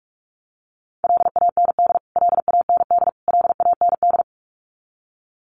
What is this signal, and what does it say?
Morse code, heard as a keyed tone, about 44 wpm, 700 Hz. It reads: LAND LAND LAND
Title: Top 500 Words Repeated 45wpm